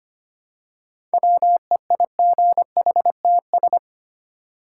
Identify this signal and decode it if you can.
Morse code, heard as a keyed tone, about 25 wpm, 700 Hz. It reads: WEIGHTS
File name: practice / 25wpm / Easy Words with Hints - 25wpm